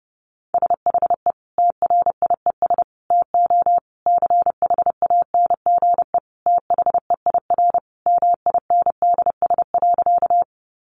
Morse code, heard as a keyed tone, about 30 wpm, 700 Hz. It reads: SHE TRIES TO CHANGE THEIR MINDS.
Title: Easy Phrases 30wpm